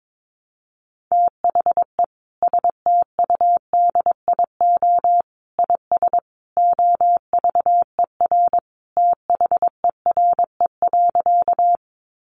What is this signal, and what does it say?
Morse code, heard as a keyed tone, about 22 wpm, 700 Hz. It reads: THE STUDIO IS OVER THERE.
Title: Sentences from Top 1200 Words 22wpm